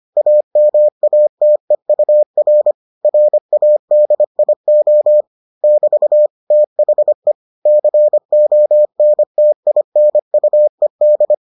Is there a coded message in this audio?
Yes: Morse code from 0.2 to 11.3 s, heard as a keyed tone, about 25 wpm, 600 Hz. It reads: AMATEUR RADIO <BT> THE CONTINUED